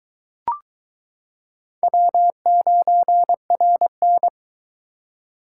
Morse code, heard as a keyed tone, about 22 wpm, 700 Hz. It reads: W9RN